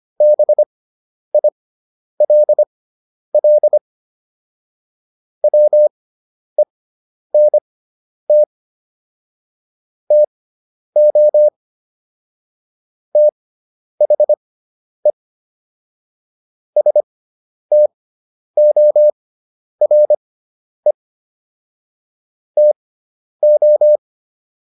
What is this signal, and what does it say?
Morse code, heard as a keyed tone, about 25 wpm, spaced as for 10 wpm, 600 Hz. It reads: BILL WENT TO THE STORE TO